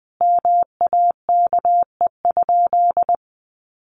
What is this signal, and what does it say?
Morse code, heard as a keyed tone, about 20 wpm, 700 Hz. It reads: MAKE?
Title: Sentences from Top 1400 Words 20wpm